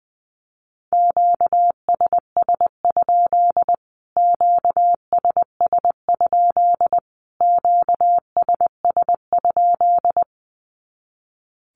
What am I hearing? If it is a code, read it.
Morse code, heard as a keyed tone, about 20 wpm, 700 Hz. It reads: QSS? QSS? QSS?
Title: Q-Codes Repeated 3x 20wpm